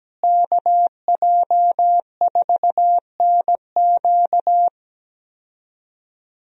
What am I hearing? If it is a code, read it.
Morse code, heard as a keyed tone, about 17 wpm, 700 Hz. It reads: KJ4NQ
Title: USA Call Signs - 5 Characters - Repeated 3x - 17wpm